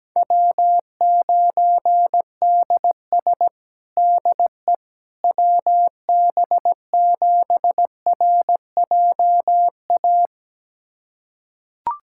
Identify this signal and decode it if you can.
Morse code, heard as a keyed tone, about 17 wpm, 700 Hz. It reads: W9DS DE WB7RJA
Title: USA Callsign DE Callsign 17wpm